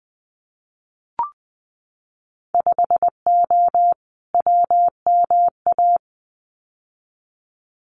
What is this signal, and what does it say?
Morse code, heard as a keyed tone, about 20 wpm, 700 Hz. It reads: 5O WMA
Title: Winter Field Day Exchanges Encore 20wpm